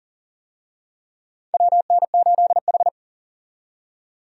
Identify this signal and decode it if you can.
Morse code, heard as a keyed tone, about 40 wpm, 700 Hz. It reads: WN8H
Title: USA Call Signs - 4 Characters - 40wpm